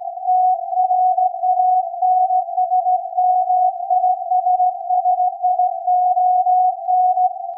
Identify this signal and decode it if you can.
Morse code, heard as a keyed tone, about 17 wpm, 750 Hz. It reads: TRANSMISSION
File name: iz7slz / VLF / SAQ 24-10-2023